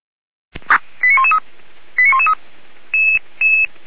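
Five-tone selective call (ZVEI), heard as quick runs of five tones, tones of about 70 ms. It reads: 89113 89113
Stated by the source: ZVEI five-tone fire brigade alert call received by SDR.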